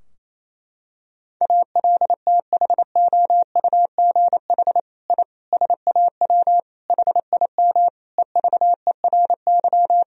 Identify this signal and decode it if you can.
Morse code, heard as a keyed tone, about 28 wpm, 700 Hz. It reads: ALTHOUGH I SAW HIM EVERY
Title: Sentences from Top 1000 Words - Review Part 3 - 28wpm